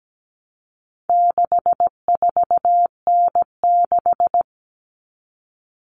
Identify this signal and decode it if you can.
Morse code, heard as a keyed tone, about 17 wpm, 700 Hz. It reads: 64N6
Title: All Cut Numbers - 4-Digit 17wpm